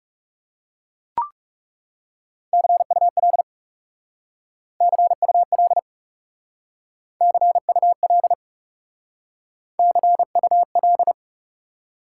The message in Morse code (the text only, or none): CUL CUL CUL CUL